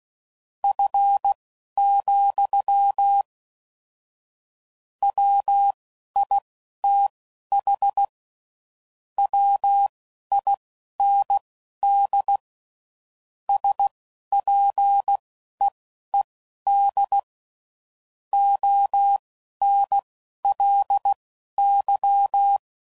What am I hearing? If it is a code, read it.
Morse code, heard as a keyed tone, about 16 wpm, spaced as for 11 wpm, 800 Hz. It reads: F, WITH WIND SPEED ONLY
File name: CW Code Groups @10WPM / Weather Sets / Weather 4 28